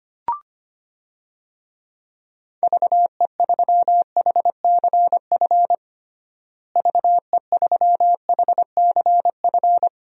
Morse code, heard as a keyed tone, about 25 wpm, 700 Hz. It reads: VE3HCF VE3HCF
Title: DXCC-100 Call Signs - 6 Characters - Repeated 3x - 25wpm